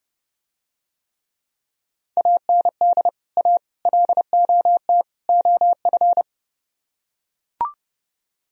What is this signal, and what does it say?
Morse code, heard as a keyed tone, about 30 wpm, 700 Hz. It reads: AND A LOT OF